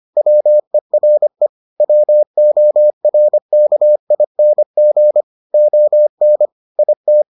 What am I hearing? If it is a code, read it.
Morse code, heard as a keyed tone, about 25 wpm, 600 Hz. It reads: WERE WORKING ON IT